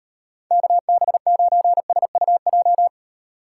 Morse code, heard as a keyed tone, about 38 wpm, 700 Hz. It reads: KB9SUJ